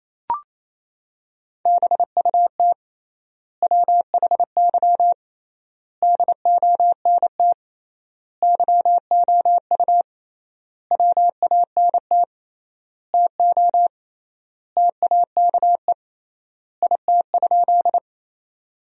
Morse code, heard as a keyed tone, about 28 wpm, 700 Hz. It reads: BUT WHY DONT YOU WANT TO TAKE IT?
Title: 3x Word-Spacing - Sentences from Top 300 Words - 28wpm